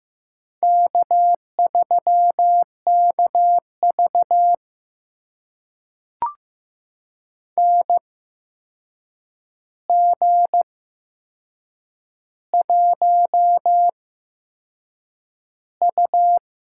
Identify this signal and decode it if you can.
Morse code, heard as a keyed tone, about 15 wpm, spaced as for 6 wpm, 700 Hz. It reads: K3KV NG1U
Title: USA Call Signs - 4 Characters - 8x Character Spacing - 15wpm